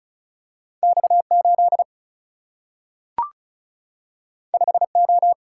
Morse code, heard as a keyed tone, about 36 wpm, 700 Hz. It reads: X 8 5 O
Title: Single-Letter Single-Number 35wpm